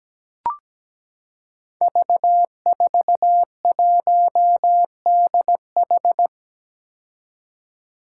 Morse code, heard as a keyed tone, about 17 wpm, 700 Hz. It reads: V41DH
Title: Five Challenging Letter-Number Combinations 17wpm